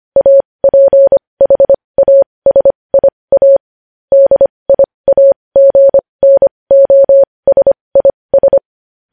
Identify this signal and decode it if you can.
Morse code, heard as a keyed tone, about 25 wpm, spaced as for 20 wpm, 550 Hz. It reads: APHASIA DIAGNOSIS